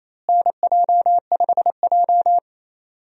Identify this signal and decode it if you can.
Morse code, heard as a keyed tone, about 28 wpm, 700 Hz. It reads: NJ5J
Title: USA Call Signs - 4 Characters - Repeated 3x - 28wpm